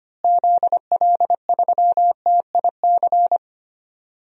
Morse code, heard as a keyed tone, about 25 wpm, 700 Hz. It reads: ZL3TIC
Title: DXCC-100 Call Signs - 6 Characters - Repeated 2x - 25wpm